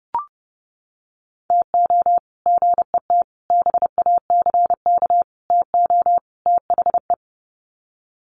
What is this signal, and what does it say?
Morse code, heard as a keyed tone, about 30 wpm, 700 Hz. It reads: TO GET BACK TO THE
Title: Sets of 5 Words Encore 30wpm